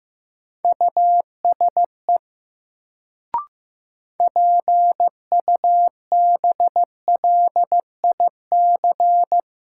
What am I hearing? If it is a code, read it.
Morse code, heard as a keyed tone, about 15 wpm, 700 Hz. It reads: USE PUBLIC
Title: Sets of 2 Words - New Word Drill - 15wpm